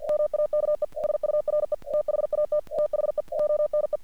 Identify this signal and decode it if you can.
Morse code, heard as a keyed tone, about 45 wpm, 600 Hz. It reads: MAKE SURE THAT THE ONE
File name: archived / CRSnet